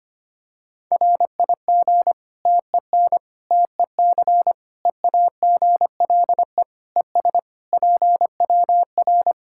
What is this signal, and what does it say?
Morse code, heard as a keyed tone, about 25 wpm, 700 Hz. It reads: RIG TEN TEC EAGLE ES PWR
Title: Practice QSO - Exchange 2 - Info - 25wpm